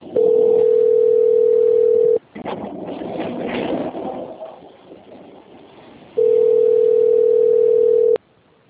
Ringback tone (North America), 440 + 480 Hz.
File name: out- - - - - .48248